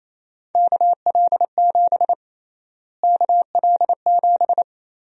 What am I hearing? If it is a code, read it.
Morse code, heard as a keyed tone, about 28 wpm, 700 Hz. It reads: KL7 KL7
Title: USA Call Sign Prefixes - Repeated 2x - 28wpm